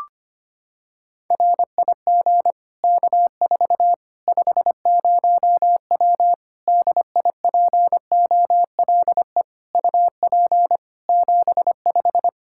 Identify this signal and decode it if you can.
Morse code, heard as a keyed tone, about 25 wpm, 700 Hz. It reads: RIG K4 50W DIPOLE UP 75
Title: Practice QSO - Exchange 2 - Info - 25wpm